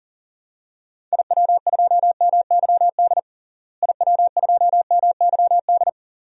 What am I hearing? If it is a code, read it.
Morse code, heard as a keyed tone, about 40 wpm, 700 Hz. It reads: IW2MYD IW2MYD
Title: DXCC-100 Call Signs - 6 Characters - Repeated 2x - 40wpm